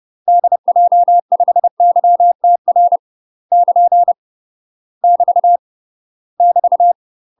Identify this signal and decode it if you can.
Morse code, heard as a keyed tone, about 30 wpm, 700 Hz. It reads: DJ5YTR <KN> <BT> <BT>
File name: UK MASTER QSOs TRAINING / 30WPM / UK MASTER QSOs 0001